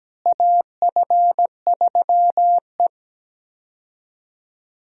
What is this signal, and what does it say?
Morse code, heard as a keyed tone, about 17 wpm, 700 Hz. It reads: AF3E
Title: USA Call Signs - 4 Characters - Repeated 3x - 17wpm